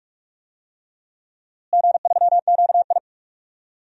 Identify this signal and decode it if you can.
Morse code, heard as a keyed tone, about 41 wpm, 700 Hz. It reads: G3QI
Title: DXCC-100 Call Signs - 4 Characters - Repeated 3x - 45wpm